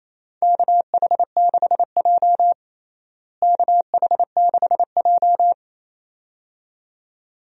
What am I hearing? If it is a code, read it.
Morse code, heard as a keyed tone, about 28 wpm, 700 Hz. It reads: KH6J KH6J
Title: DXCC-100 Call Signs - 4 Characters - Repeated 2x - 28wpm